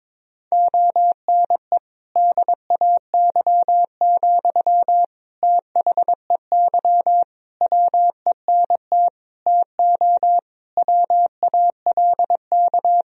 Morse code, heard as a keyed tone, about 22 wpm, 700 Hz. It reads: ONE DAY, THEY WENT TO WALK
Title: Sentences from Top 1000 Words - Review Part 2 - 22wpm